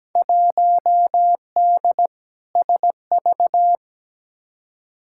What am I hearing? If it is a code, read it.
Morse code, heard as a keyed tone, about 17 wpm, 700 Hz. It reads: 1D SV
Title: Field Day Exchanges - 17wpm